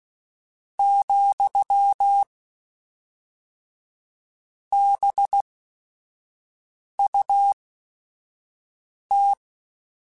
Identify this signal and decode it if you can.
Morse code, heard as a keyed tone, about 16 wpm, spaced as for 5 wpm, 800 Hz. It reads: , BUT